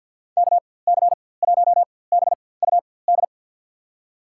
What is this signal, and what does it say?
Morse code, heard as a keyed tone, about 52 wpm, spaced as for 23 wpm, 700 Hz. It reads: KC1BUD
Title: USA Call Signs - 4x Character Spacing - 50wpm